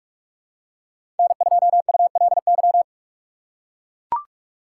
Morse code, heard as a keyed tone, about 44 wpm, 700 Hz. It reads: N2ULY
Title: USA Call Signs - 5 Characters - 6x Character Spacing - 45wpm